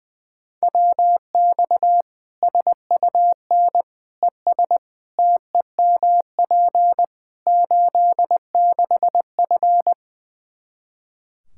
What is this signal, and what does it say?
Morse code, heard as a keyed tone, about 20 wpm, 700 Hz. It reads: WX SUN ES TEMP 86F
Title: K1USN SST QSO Practice 20wpm